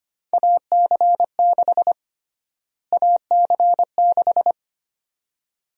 Morse code, heard as a keyed tone, about 25 wpm, 700 Hz. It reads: AC6 AC6
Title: USA Call Sign Prefixes - Repeated 2x - 25wpm